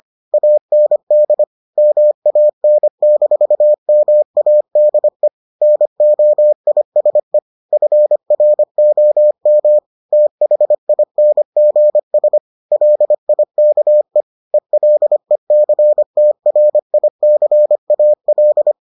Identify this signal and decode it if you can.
Morse code, heard as a keyed tone, about 25 wpm, 600 Hz. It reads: AND MAN-MADE NOISE FROM THINGS LIKE ELECTRICAL